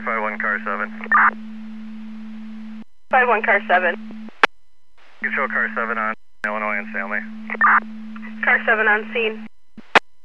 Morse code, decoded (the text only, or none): none